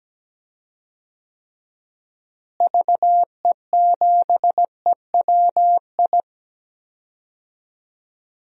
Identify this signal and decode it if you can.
Morse code, heard as a keyed tone, about 17 wpm, 700 Hz. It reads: VE7EWI
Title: DXCC-100 Call Signs - 6 Characters - Repeated 2x - 17wpm